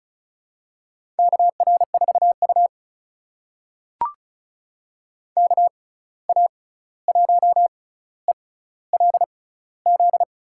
Morse code, heard as a keyed tone, about 36 wpm, spaced as for 16 wpm, 700 Hz. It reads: KR4U KA1ELZ
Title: USA Call Signs - 6x Character Spacing - 35wpm